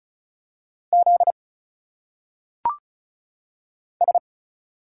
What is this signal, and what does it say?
Morse code, heard as a keyed tone, about 36 wpm, 700 Hz. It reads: Z S
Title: Single Letters - Phonetic Pronunciation 35wpm